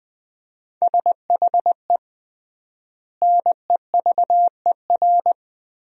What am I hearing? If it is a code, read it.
Morse code, heard as a keyed tone, about 20 wpm, 700 Hz. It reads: SHE NEVER
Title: 3x Word-Spacing - Sentences from Top 300 Words - 20wpm